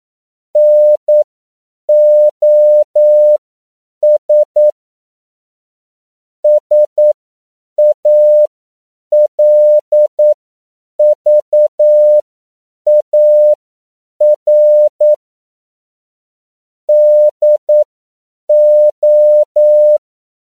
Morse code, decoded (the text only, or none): NOS SALVAR DO